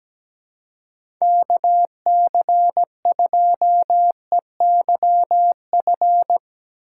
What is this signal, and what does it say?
Morse code, heard as a keyed tone, about 17 wpm, 700 Hz. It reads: KC2EYF
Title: USA Call Signs - 4x Character Spacing - 17wpm